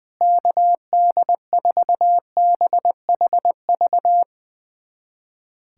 Morse code, heard as a keyed tone, about 20 wpm, 700 Hz. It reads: KD4BHV
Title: USA Callsign DE Callsign 20wpm